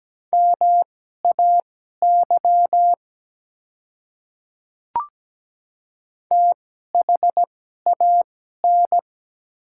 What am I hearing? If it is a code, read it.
Morse code, heard as a keyed tone, about 17 wpm, spaced as for 12 wpm, 700 Hz. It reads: MAY THAN
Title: 2x Character Spacing - Top 100 Words - 17wpm